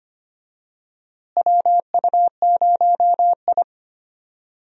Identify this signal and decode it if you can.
Morse code, heard as a keyed tone, about 25 wpm, 700 Hz. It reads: WU0I
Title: USA Call Signs - 4 Characters - Repeated 3x - 25wpm